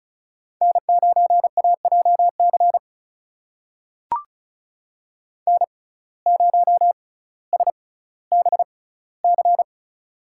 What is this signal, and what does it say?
Morse code, heard as a keyed tone, about 36 wpm, spaced as for 18 wpm, 700 Hz. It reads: N9AJC N0SBC
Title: USA Call Signs - 5 Characters - 6x Character Spacing - 35wpm